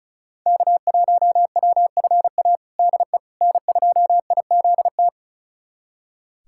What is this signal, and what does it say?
Morse code, heard as a keyed tone, about 34 wpm, 700 Hz. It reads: K1WFA DE N2IZT
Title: USA Callsign DE Callsign - 5 Characters - 35wpm